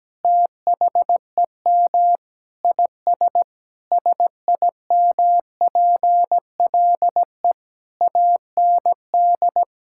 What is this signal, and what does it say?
Morse code, heard as a keyed tone, about 17 wpm, 700 Hz. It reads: THEM IS SIMPLE AND
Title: Sentences from Top 1600 Words 17wpm